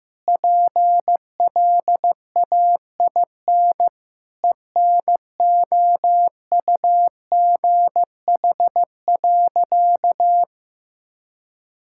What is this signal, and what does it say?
Morse code, heard as a keyed tone, about 15 wpm, 700 Hz. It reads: PLAIN ENOUGH.
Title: Sentences from Top 1000 Words - Review Part 5 - 15wpm